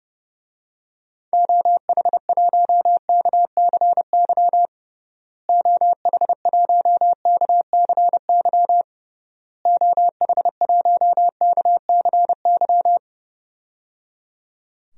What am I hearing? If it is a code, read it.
Morse code, heard as a keyed tone, about 30 wpm, 700 Hz. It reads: OH1KCY OH1KCY OH1KCY
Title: DXCC-100 Call Signs - 6 Characters - Repeated 3x - 30wpm